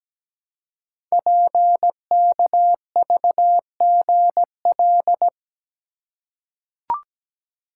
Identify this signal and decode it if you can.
Morse code, heard as a keyed tone, about 17 wpm, 700 Hz. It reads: PKVGL
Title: Code Groups - 5 Letters - 17wpm